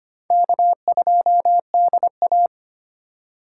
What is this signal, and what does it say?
Morse code, heard as a keyed tone, about 31 wpm, 700 Hz. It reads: K2DA